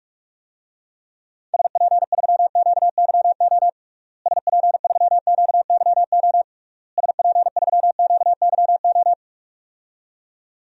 Morse code, heard as a keyed tone, about 46 wpm, 700 Hz. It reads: SP3QYO SP3QYO SP3QYO
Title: DXCC-100 Call Signs - 6 Characters - Repeated 3x - 45wpm